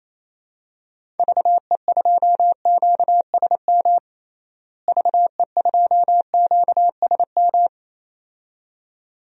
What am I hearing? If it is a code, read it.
Morse code, heard as a keyed tone, about 27 wpm, 700 Hz. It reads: VE2QSM VE2QSM
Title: DXCC-100 Call Signs - 6 Characters - Repeated 2x - 28wpm